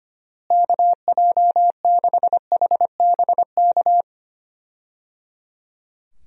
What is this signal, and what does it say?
Morse code, heard as a keyed tone, about 25 wpm, 700 Hz. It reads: KJ6HBK
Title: USA Call Signs - 6 Characters - Repeated 3x - 25wpm